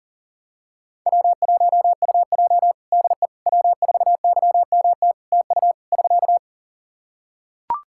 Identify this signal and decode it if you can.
Morse code, heard as a keyed tone, about 40 wpm, 700 Hz. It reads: W1UJ DE W4YMT TU <SK>
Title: Practice QSO - Exchange 4 - Ending - 40wpm